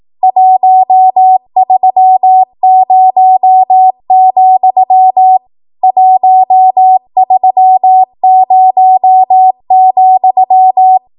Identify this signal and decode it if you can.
Morse code, heard as a keyed tone, about 18 wpm, 750 Hz. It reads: 130, 130,
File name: Morse / 18WPM